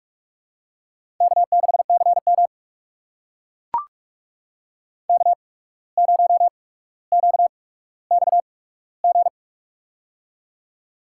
Morse code, heard as a keyed tone, about 44 wpm, spaced as for 18 wpm, 700 Hz. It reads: K6CM K0QXG
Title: USA Call Signs - 8x Character Spacing - 45wpm